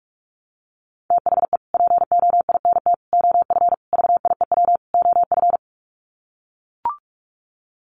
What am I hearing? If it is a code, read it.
Morse code, heard as a keyed tone, about 44 wpm, 700 Hz. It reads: THE POINT OF VIEW OF
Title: Sets of 5 Words 45wpm